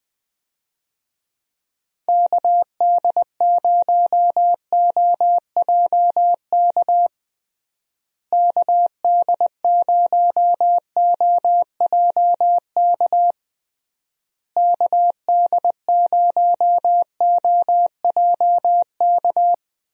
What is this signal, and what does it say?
Morse code, heard as a keyed tone, about 20 wpm, 700 Hz. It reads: KD0OJK KD0OJK KD0OJK
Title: USA Call Signs - 6 Characters - Repeated 3x - 20wpm